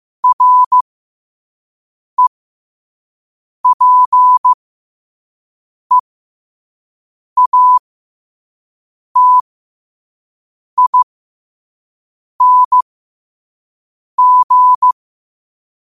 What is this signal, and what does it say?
Morse code, heard as a keyed tone, about 15 wpm, spaced as for 5 wpm, 1000 Hz. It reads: REPEATING
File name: qotdfiles / 5wpm